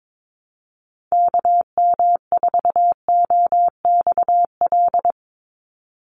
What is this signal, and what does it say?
Morse code, heard as a keyed tone, about 22 wpm, 700 Hz. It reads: KM4OXL
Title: USA Call Signs 22wpm